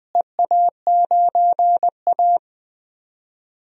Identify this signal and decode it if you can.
Morse code, heard as a keyed tone, about 20 wpm, 700 Hz. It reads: EA9A
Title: DXCC-100 Call Signs - 4 Characters - Repeated 2x - 20wpm